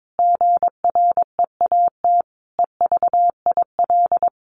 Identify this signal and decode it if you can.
Morse code, heard as a keyed tone, about 22 wpm, 700 Hz. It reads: GREAT EVIL
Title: A Sound Of Thunder 22wpm